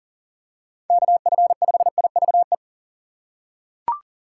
Morse code, heard as a keyed tone, about 39 wpm, 700 Hz. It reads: KF5IVE
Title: USA Call Signs - 6 Characters - 2x Character Spacing - 40wpm